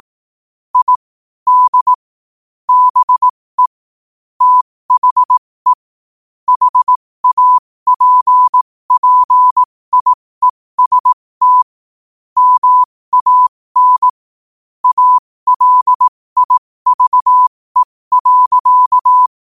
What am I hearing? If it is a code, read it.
Morse code, heard as a keyed tone, about 18 wpm, spaced as for 15 wpm, 1000 Hz. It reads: I D BE THE HAPPIEST MAN ALIVE.
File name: qotdfiles / 15wpm